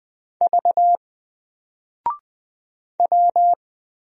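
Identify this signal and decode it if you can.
Morse code, heard as a keyed tone, about 20 wpm, 700 Hz. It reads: V W
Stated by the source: Single letter and number Morse drill at 20 wpm, from N4LCD.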